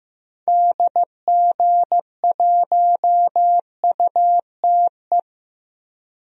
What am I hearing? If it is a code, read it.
Morse code, heard as a keyed tone, about 15 wpm, 700 Hz. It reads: DG1UTE
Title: DXCC-100 Call Signs - 6 Characters - 15wpm